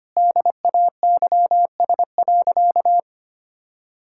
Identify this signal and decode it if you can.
Morse code, heard as a keyed tone, about 25 wpm, 700 Hz. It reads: DAYS.
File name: practice / 25wpm / Sentences with 4-Word Phrases - 25wpm